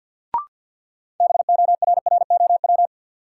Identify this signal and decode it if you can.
Morse code, heard as a keyed tone, about 50 wpm, 700 Hz. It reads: BORROW
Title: Sentences from Top 1500 Words - New Word Drill - 50wpm